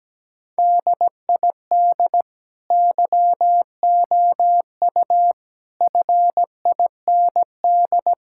Morse code, heard as a keyed tone, about 17 wpm, 700 Hz. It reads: DID YOU FIND
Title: Sentences from Top 1000 Words - Review Part 6 - 17wpm